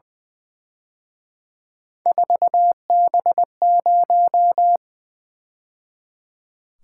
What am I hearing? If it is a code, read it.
Morse code, heard as a keyed tone, about 24 wpm, 700 Hz. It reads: 4B0
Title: Speed-Racing - Three Letter-Number Combinations - 20wpm